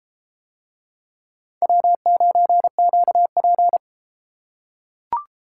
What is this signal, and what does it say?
Morse code, heard as a keyed tone, about 32 wpm, 700 Hz. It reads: W9QP